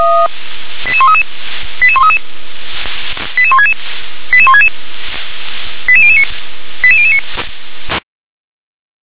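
Five-tone selective call (ZVEI), heard as quick runs of five tones, tones of about 70 ms. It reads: 88133 88133 88166 88166 88009 88009
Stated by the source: ZVEI five-tone selective call received on an SDR.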